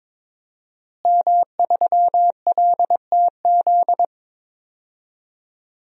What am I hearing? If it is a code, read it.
Morse code, heard as a keyed tone, about 22 wpm, 700 Hz. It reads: M3LTZ
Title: DXCC-100 Call Signs - 5 Characters - Repeated 3x - 22wpm